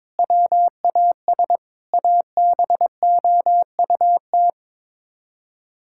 Morse code, heard as a keyed tone, about 21 wpm, 700 Hz. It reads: WAS ABOUT
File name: practice / 22wpm / Sets of 8 Words - 22wpm